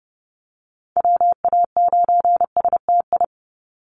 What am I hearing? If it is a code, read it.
Morse code, heard as a keyed tone, about 30 wpm, 700 Hz. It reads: WA9STI
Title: NA SOTA Activator Callsigns 30wpm